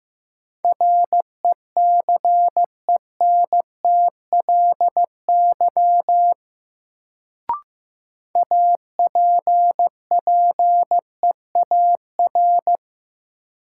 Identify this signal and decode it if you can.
Morse code, heard as a keyed tone, about 15 wpm, 700 Hz. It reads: RECENTLY APPEAR
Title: Sentences from Top 1200 Words - New Word Drill Alt - 15wpm